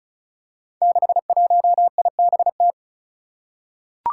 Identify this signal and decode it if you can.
Morse code, heard as a keyed tone, about 36 wpm, 700 Hz. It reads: 61IBT
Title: Five Challenging Letter-Number Combinations 35wpm